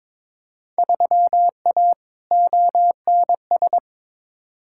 Morse code, heard as a keyed tone, about 22 wpm, 700 Hz. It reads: 3A ONS
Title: Field Day Exchanges - 22wpm